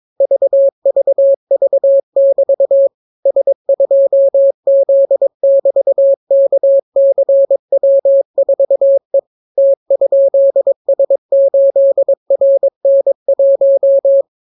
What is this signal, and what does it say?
Morse code, heard as a keyed tone, about 22 wpm, 550 Hz. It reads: VVV<BT> S2Z<BT>KCW4E T?S8RN1